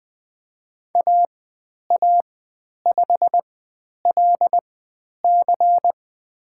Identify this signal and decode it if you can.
Morse code, heard as a keyed tone, about 20 wpm, spaced as for 10 wpm, 700 Hz. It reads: AA5LC
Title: 095 Lesson 16g - Letter c - Call Signs 20wpm